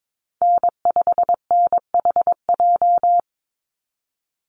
Morse code, heard as a keyed tone, about 22 wpm, 700 Hz. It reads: N5NHJ
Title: Call Signs Repeated 2x - 22wpm